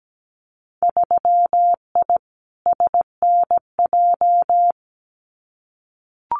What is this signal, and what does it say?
Morse code, heard as a keyed tone, about 17 wpm, 700 Hz. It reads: 3I SNJ
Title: Winter Field Day Exchanges Encore 17wpm